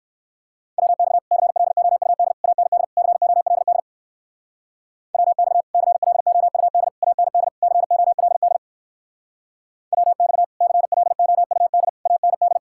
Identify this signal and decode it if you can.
Morse code, heard as a keyed tone, about 59 wpm, 700 Hz. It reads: WX CLOUD AND COLD WX CLOUD AND COLD WX CLOUD AND